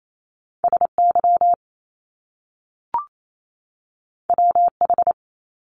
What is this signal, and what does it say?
Morse code, heard as a keyed tone, about 28 wpm, 700 Hz. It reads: SY WH
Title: Top Two Letter Combinations in Words 28wpm